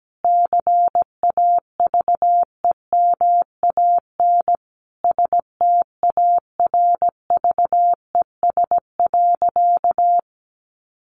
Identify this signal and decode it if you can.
Morse code, heard as a keyed tone, about 17 wpm, 700 Hz. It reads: CAVEMAN STARVES.
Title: A Sound Of Thunder 17wpm